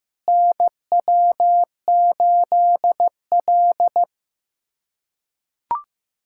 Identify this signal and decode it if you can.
Morse code, heard as a keyed tone, about 16 wpm, 700 Hz. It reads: NW8L
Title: USA Callsign DE Callsign - 4 Characters - 15wpm